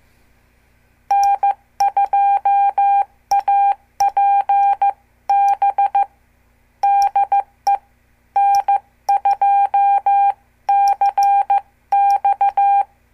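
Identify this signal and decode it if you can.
Morse code, heard as a keyed tone, about 15 wpm, 800 Hz. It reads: N2APB DE N2CX
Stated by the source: Example CW QSO between N2APB and N2CX from a CWTD Morse lesson.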